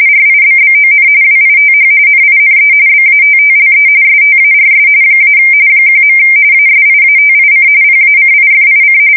RTTY, about 45.45 baud. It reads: RIBBON WITH NINE BATTLE STARS - 1. ASIATIC PACIFIC
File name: fsk170 / battleshipnj